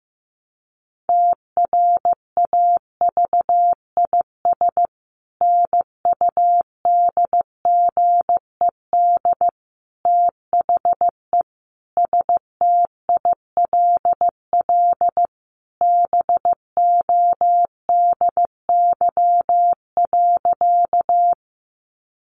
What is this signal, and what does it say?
Morse code, heard as a keyed tone, about 15 wpm, 700 Hz. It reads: TRAVIS NUDGED THE STILL BODY.
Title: A Sound Of Thunder 15wpm